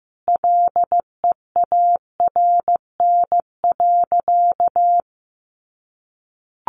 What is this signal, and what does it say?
Morse code, heard as a keyed tone, about 15 wpm, 700 Hz. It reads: LEARN.
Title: Sentences from Top 1400 Words 15wpm